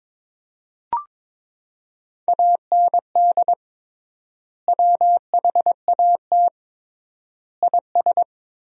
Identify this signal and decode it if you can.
Morse code, heard as a keyed tone, about 22 wpm, 700 Hz. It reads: AND WHAT IS
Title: 3x Word-Spacing - Sentences from Top 100 Words - 22wpm